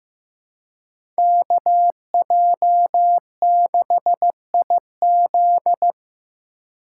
Morse code, heard as a keyed tone, about 15 wpm, 700 Hz. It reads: KJ6IZ
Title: USA Call Signs - 2x Character Spacing - 15wpm